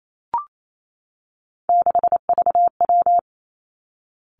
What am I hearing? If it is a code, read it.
Morse code, heard as a keyed tone, about 28 wpm, 700 Hz. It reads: 6VW
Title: Three Challenging Letter-Number Combinations 28wpm